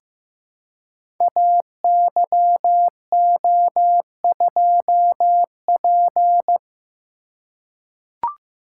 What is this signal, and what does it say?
Morse code, heard as a keyed tone, about 15 wpm, 700 Hz. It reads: AYO2P
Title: Code Groups - 5 Letter:Numbers - 15wpm